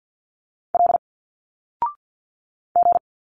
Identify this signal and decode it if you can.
Morse code, heard as a keyed tone, about 50 wpm, 700 Hz. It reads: L G
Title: Single Letters 50wpm